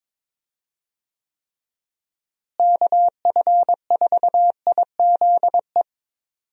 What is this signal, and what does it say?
Morse code, heard as a keyed tone, about 22 wpm, 700 Hz. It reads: KF4IZE
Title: USA Call Signs - 6 Characters - Repeated 2x - 22wpm